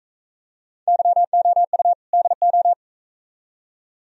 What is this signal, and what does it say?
Morse code, heard as a keyed tone, about 42 wpm, 700 Hz. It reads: YOU DO